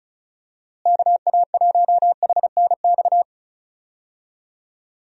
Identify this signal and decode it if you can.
Morse code, heard as a keyed tone, about 36 wpm, 700 Hz. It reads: KA1HNX
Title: USA Call Signs - 6 Characters - Repeated 2x - 35wpm